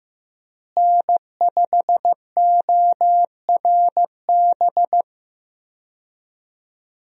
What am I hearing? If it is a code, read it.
Morse code, heard as a keyed tone, about 15 wpm, 700 Hz. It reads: N5ORB
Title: USA Call Signs - 5 Characters - Repeated 2x - 15wpm